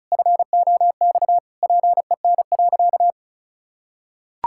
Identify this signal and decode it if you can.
Morse code, heard as a keyed tone, about 36 wpm, 700 Hz. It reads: FOX PEN.
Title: Pangram Challenge v2 - 35wpm